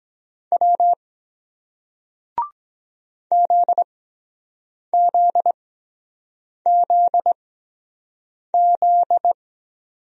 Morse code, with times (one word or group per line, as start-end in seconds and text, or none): 0.5-0.9 W
3.3-9.3 ZZZZ